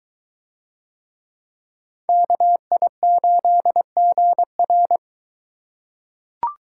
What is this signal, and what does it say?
Morse code, heard as a keyed tone, about 21 wpm, 700 Hz. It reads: KI8GR